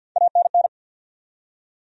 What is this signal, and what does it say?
Morse code, heard as a keyed tone, about 52 wpm, 700 Hz. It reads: ANN
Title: USA Common Names - Max 3 Letters - 50wpm